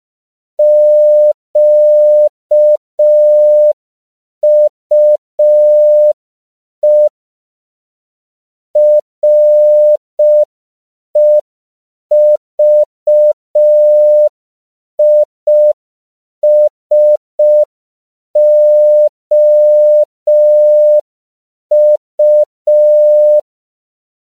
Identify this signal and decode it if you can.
Morse code, heard as a keyed tone, about 5 wpm, 600 Hz. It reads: QUE REVISOU
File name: CW / 5PPM